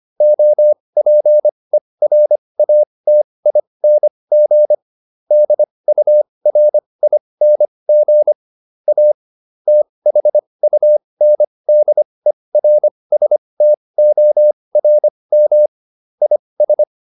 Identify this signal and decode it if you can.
Morse code, heard as a keyed tone, about 25 wpm, spaced as for 20 wpm, 600 Hz. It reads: OPERATING DURING A THUNDERSTORM IS